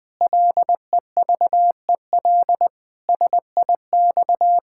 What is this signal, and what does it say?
Morse code, heard as a keyed tone, about 21 wpm, 700 Hz. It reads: LEVEL SIX
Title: Pangram Challenge v2 - 20wpm